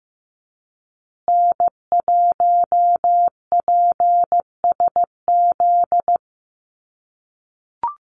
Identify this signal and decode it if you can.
Morse code, heard as a keyed tone, about 15 wpm, 700 Hz. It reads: N1PSZ
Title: USA Call Signs Repeated 3x 15wpm